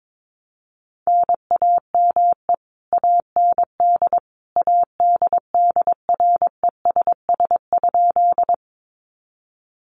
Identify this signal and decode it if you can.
Morse code, heard as a keyed tone, about 22 wpm, 700 Hz. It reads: NAME AND ADDRESS?
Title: Sentences from Top 1000 Words 22wpm